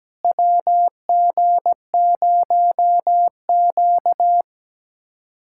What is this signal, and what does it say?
Morse code, heard as a keyed tone, about 17 wpm, 700 Hz. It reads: WG0Q
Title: USA Callsign DE Callsign - 4 Characters - 17wpm